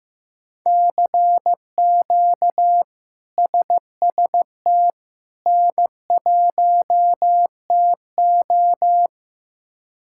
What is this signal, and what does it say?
Morse code, heard as a keyed tone, about 15 wpm, 700 Hz. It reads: CQ SST N1TO
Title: K1USN SST QSO Practice - 15wpm